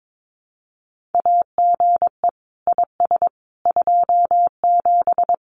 Morse code, heard as a keyed tone, about 22 wpm, 700 Hz. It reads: AGE IS 27
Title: Common QSO Phrases Encore 22wpm